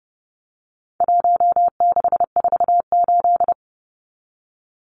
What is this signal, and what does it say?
Morse code, heard as a keyed tone, about 30 wpm, 700 Hz. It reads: 1648
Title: Numbers - 4-Digit 30wpm